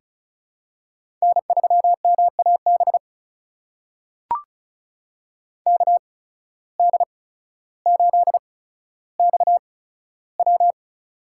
Morse code, heard as a keyed tone, about 34 wpm, spaced as for 15 wpm, 700 Hz. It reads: N3MAB KD8XW
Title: USA Call Signs - 5 Characters - 8x Character Spacing - 35wpm